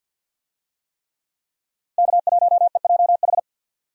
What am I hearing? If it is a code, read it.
Morse code, heard as a keyed tone, about 51 wpm, 700 Hz. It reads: K1EJH
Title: USA Call Signs - 5 Characters - 50wpm